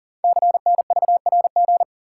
Morse code, heard as a keyed tone, about 42 wpm, 700 Hz. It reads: CNVRG
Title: Code Groups - 5 Letters - 40wpm